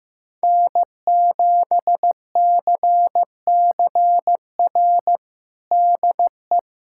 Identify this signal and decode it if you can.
Morse code, heard as a keyed tone, about 15 wpm, 700 Hz. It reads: N7CCR DE
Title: USA Callsign DE Callsign - 5 Characters - 15wpm